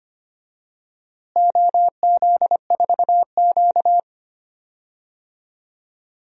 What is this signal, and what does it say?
Morse code, heard as a keyed tone, about 25 wpm, 700 Hz. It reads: OZ4Q
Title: DXCC-100 Call Signs - 4 Characters - Repeated 2x - 25wpm